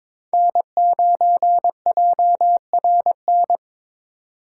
Morse code, heard as a keyed tone, about 22 wpm, 700 Hz. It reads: N9JRN
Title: USA Call Signs - 5 Characters - 3x Character Spacing - 22wpm